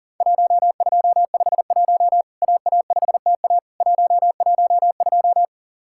Morse code, heard as a keyed tone, about 40 wpm, 700 Hz. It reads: 1251 AA5TA 112
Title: Worked All Europe DX Contest - QTC Exchange - 40wpm